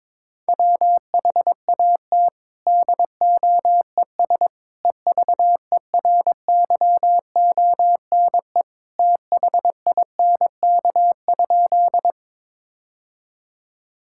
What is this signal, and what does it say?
Morse code, heard as a keyed tone, about 22 wpm, 700 Hz. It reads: WHAT DOES EVERYONE THINK?
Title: Sentences from Top 1000 Words - Review Part 5 - 22wpm